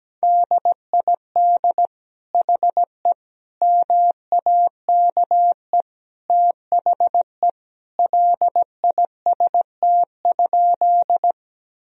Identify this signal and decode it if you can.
Morse code, heard as a keyed tone, about 17 wpm, 700 Hz. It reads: DID HE MAKE THE LIST?
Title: Sentences from Top 1000 Words - Review Part 2 - 17wpm